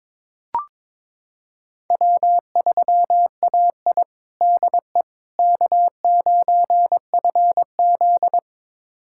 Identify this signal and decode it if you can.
Morse code, heard as a keyed tone, about 22 wpm, 700 Hz. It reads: W3AI DE K9FZ
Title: USA Callsign DE Callsign - 4 Characters - 22wpm